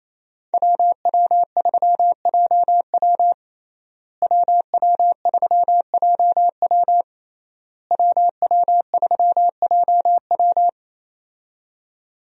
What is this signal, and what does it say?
Morse code, heard as a keyed tone, about 28 wpm, 700 Hz. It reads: WW3JW WW3JW WW3JW
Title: USA Call Signs - 5 Characters - Repeated 3x - 28wpm